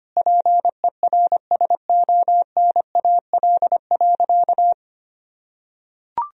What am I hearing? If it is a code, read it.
Morse code, heard as a keyed tone, about 25 wpm, 700 Hz. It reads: PERSONAL.
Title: Sentences from Top 1000 Words - Review Part 4 - 25wpm